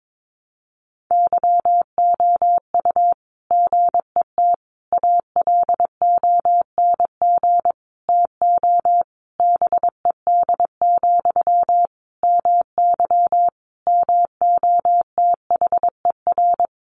Morse code, heard as a keyed tone, about 22 wpm, 700 Hz. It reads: YOU GET ALONG TO BED, MY MOTHER
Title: Sentences from Top 700 Words 22wpm